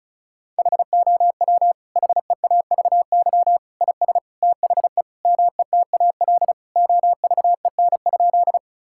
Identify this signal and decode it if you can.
Morse code, heard as a keyed tone, about 36 wpm, 700 Hz. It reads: HOW HEAVY IS THE METAL OVEN?
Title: Sentences from Top 1000 Words - Review Part 6 - 35wpm